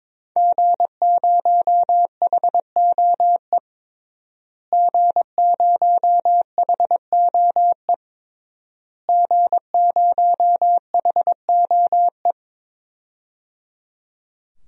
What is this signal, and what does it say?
Morse code, heard as a keyed tone, about 22 wpm, 700 Hz. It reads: G0HOE G0HOE G0HOE
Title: DXCC-100 Call Signs - 5 Characters - Repeated 3x - 22wpm